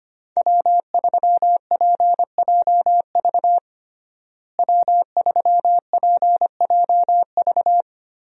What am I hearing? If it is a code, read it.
Morse code, heard as a keyed tone, about 25 wpm, 700 Hz. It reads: W3PJV W3PJV
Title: USA Call Signs - 5 Characters - Repeated 3x - 25wpm